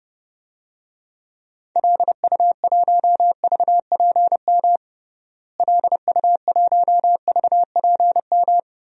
Morse code, heard as a keyed tone, about 30 wpm, 700 Hz. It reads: LU1VPM LU1VPM
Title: DXCC-100 Call Signs - 6 Characters - Repeated 2x - 30wpm